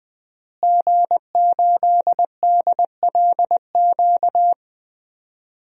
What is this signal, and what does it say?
Morse code, heard as a keyed tone, about 20 wpm, 700 Hz. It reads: G8DLQ
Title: DXCC-100 Call Signs - 5 Characters - Repeated 2x - 20wpm